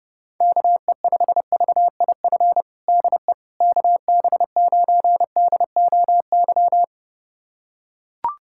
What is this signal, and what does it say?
Morse code, heard as a keyed tone, about 30 wpm, 700 Hz. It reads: KE5VIF DE KB9DOY
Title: USA Callsign DE Callsign - 6 Characters - 30wpm